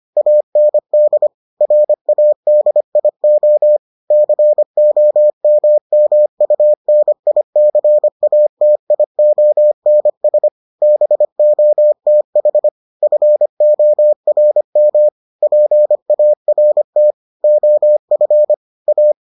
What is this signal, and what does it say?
Morse code, heard as a keyed tone, about 25 wpm, 600 Hz. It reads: AND RADIO COMMUNICATIONS BOTH FORM PART OF A